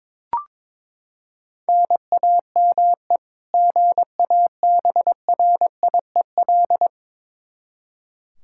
Morse code, heard as a keyed tone, about 22 wpm, 700 Hz. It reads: NAME GABRIEL
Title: K1USN SST QSO Practice 22wpm